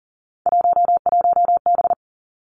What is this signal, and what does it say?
Morse code, heard as a keyed tone, about 40 wpm, 700 Hz. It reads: 11B
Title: Three Challenging Letter-Number Combinations 40wpm